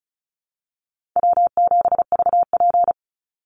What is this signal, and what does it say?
Morse code, heard as a keyed tone, about 34 wpm, 700 Hz. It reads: W7VP
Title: Call Signs Repeated 2x - 35wpm